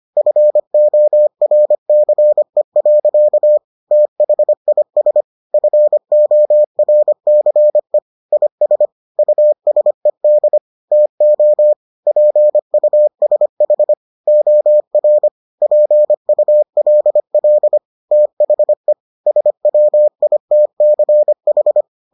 Morse code, heard as a keyed tone, about 25 wpm, 600 Hz. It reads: FORCE. THIS FORCE IS USED TO PUSH OR PULL THE SWITCH